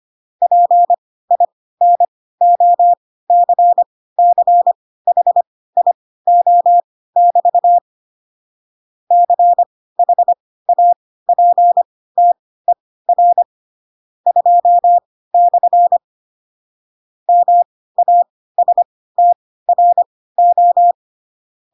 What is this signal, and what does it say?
Morse code, heard as a keyed tone, about 25 wpm, spaced as for 16 wpm, 700 Hz. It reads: PINOCCHIO<BT> CHAPTER 2/ MASTRO